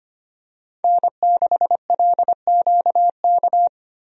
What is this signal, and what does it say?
Morse code, heard as a keyed tone, about 25 wpm, 690 Hz. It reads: N6LQK
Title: USA Call Signs - 5 Characters - 25wpm